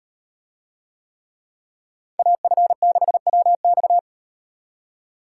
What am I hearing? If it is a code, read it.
Morse code, heard as a keyed tone, about 37 wpm, 700 Hz. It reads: AF6WX